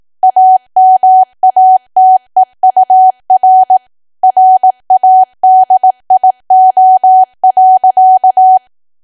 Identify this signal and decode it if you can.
Morse code, heard as a keyed tone, about 18 wpm, 750 Hz. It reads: AMATEUR RADIO.
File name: Archive / 18 WPM / 18WPM